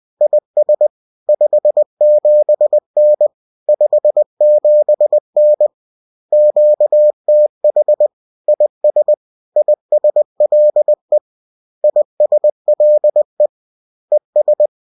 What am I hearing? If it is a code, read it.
Morse code, heard as a keyed tone, about 20 wpm, 600 Hz. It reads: IS 57N 57N QTH IS ISLE ISLE ES